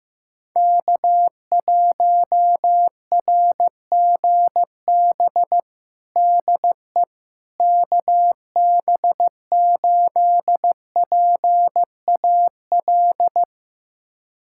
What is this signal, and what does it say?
Morse code, heard as a keyed tone, about 15 wpm, 700 Hz. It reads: K1RGB DE KB8PAL
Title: USA Callsign DE Callsign 15wpm